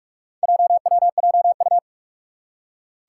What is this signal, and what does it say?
Morse code, heard as a keyed tone, about 44 wpm, 700 Hz. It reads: JWJU
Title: Four Challenging Letter-Number Combinations 45wpm